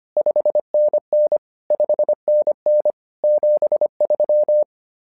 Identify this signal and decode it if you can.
Morse code, heard as a keyed tone, about 25 wpm, 600 Hz. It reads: 5NN 5NN 73
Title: SOTA QSO Practice 25wpm